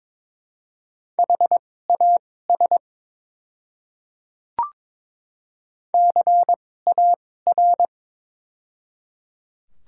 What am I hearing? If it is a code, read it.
Morse code, heard as a keyed tone, about 22 wpm, spaced as for 16 wpm, 700 Hz. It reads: HAS CAR
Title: 2x Character Spacing - Top 300 Words - 22wpm